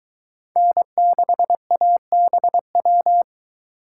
Morse code, heard as a keyed tone, about 23 wpm, 700 Hz. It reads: N6ABW